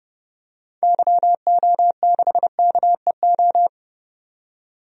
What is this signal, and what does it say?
Morse code, heard as a keyed tone, about 30 wpm, 700 Hz. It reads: YO6KEO
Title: DXCC-100 Call Signs - 6 Characters - 30wpm